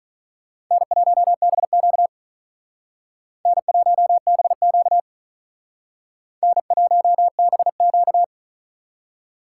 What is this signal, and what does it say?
Morse code, heard as a keyed tone, about 41 wpm, 700 Hz. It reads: N1BQ N1BQ N1BQ
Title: Speed Racing - USA 1x2 Call Signs 35wpm